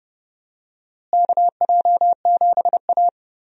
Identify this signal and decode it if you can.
Morse code, heard as a keyed tone, about 32 wpm, 700 Hz. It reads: KJ7A
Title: USA Call Signs - 4 Characters - Repeated 3x - 30wpm